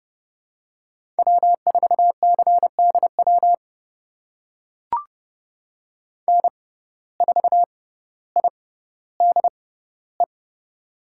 Morse code, heard as a keyed tone, about 31 wpm, spaced as for 16 wpm, 700 Hz. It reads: W4CDW N4IDE
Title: USA Call Signs - 5 Characters - 6x Character Spacing - 30wpm